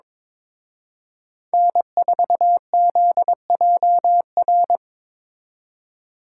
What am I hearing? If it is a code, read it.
Morse code, heard as a keyed tone, about 26 wpm, 700 Hz. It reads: N4ZJR
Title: Speed Racing - USA 1x3 Call Signs 22wpm